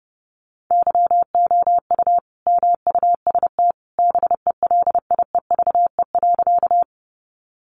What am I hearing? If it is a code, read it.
Morse code, heard as a keyed tone, about 30 wpm, 700 Hz. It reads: YOU MUST BELIEVE.
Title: Sentences from Top 1000 Words 30wpm